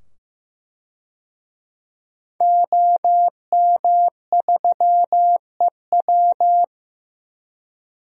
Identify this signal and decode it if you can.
Morse code, heard as a keyed tone, about 15 wpm, 700 Hz. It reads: OM3EW
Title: DXCC-100 Call Signs - 5 Characters - 15wpm